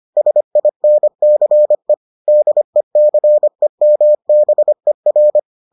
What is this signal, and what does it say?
Morse code, heard as a keyed tone, about 25 wpm, 600 Hz. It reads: SINCE DECEMBER